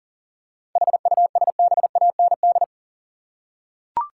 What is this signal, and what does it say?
Morse code, heard as a keyed tone, about 39 wpm, 700 Hz. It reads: HUSBAND
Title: Seven Letter Words - 40wpm